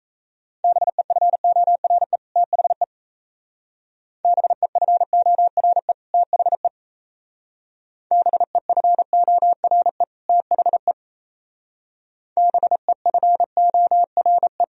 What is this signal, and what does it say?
Morse code, heard as a keyed tone, about 35 wpm, 700 Hz. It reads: BEFORE THE BEFORE THE BEFORE THE BEFORE
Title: Speed-Racing - Sets of 2 Words 28wpm